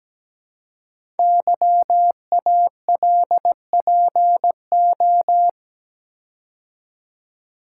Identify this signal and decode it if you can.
Morse code, heard as a keyed tone, about 17 wpm, 700 Hz. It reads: YALPO
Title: Code Groups - 5 Letters - 17wpm